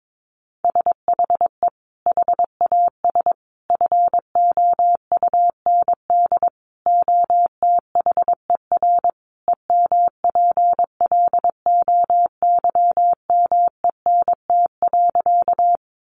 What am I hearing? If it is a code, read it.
Morse code, heard as a keyed tone, about 22 wpm, 700 Hz. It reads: SHE HAS FOUND OTHER EMPLOYMENT.
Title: Sentences from Top 1500 Words 22wpm